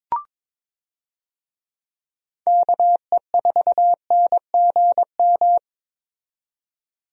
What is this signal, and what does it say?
Morse code, heard as a keyed tone, about 22 wpm, 700 Hz. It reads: KE4NGM
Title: USA Call Signs - 6 Characters - 22wpm